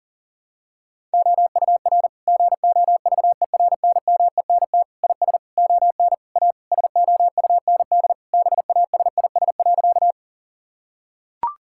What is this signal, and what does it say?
Morse code, heard as a keyed tone, about 40 wpm, 700 Hz. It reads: OUR GOVERNMENT IS ON A SOUND BASIS.
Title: Sentences from Top 1000 Words - Review Part 5 - 40wpm